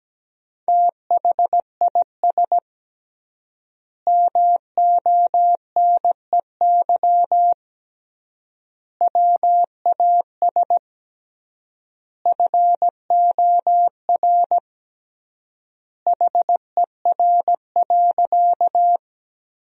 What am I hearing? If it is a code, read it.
Morse code, heard as a keyed tone, about 17 wpm, 700 Hz. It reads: THIS MONEY WAS FOR HER.
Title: 3x Word-Spacing - Sentences from Top 400 Words - 17wpm